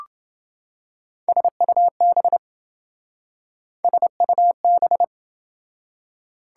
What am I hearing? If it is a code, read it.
Morse code, heard as a keyed tone, about 28 wpm, 700 Hz. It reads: SUB SUB